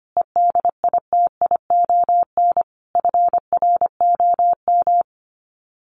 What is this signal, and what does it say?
Morse code, heard as a keyed tone, about 25 wpm, 700 Hz. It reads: EDITION FROM
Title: Sets of 5 Words Encore2 25wpm